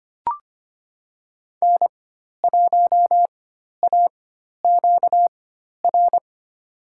Morse code, heard as a keyed tone, about 25 wpm, spaced as for 12 wpm, 700 Hz. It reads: N1AQR
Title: USA Call Signs - 5 Characters - 4x Character Spacing - 25wpm